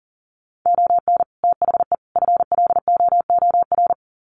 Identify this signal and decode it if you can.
Morse code, heard as a keyed tone, about 40 wpm, 700 Hz. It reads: ON THE FLOOR
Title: Sets of 3 Words 40wpm